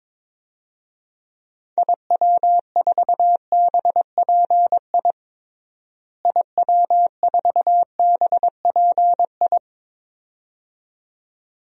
Morse code, heard as a keyed tone, about 22 wpm, 700 Hz. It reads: IW4BPI IW4BPI
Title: DXCC-100 Call Signs - 6 Characters - Repeated 2x - 22wpm